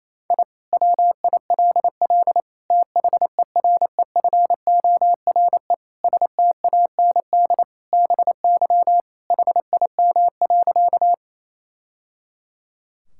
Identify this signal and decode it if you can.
Morse code, heard as a keyed tone, about 28 wpm, 700 Hz. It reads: I WILL THEREFORE STAND BY HIM.
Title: Sentences from Top 1000 Words - Review Part 5 - 28wpm